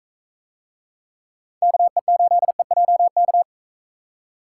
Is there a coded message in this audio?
Yes: Morse code from 1.6 to 3.4 s, heard as a keyed tone, about 44 wpm, 700 Hz. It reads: KE8EJK